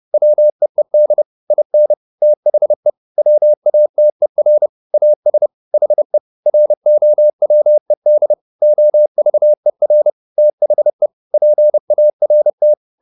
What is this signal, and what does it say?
Morse code, heard as a keyed tone, about 30 wpm, 600 Hz. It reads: WEED IN THE WATER AS HE ROWED OVER THE PART
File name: the old man and the sea / the old man and the sea 1 30wpm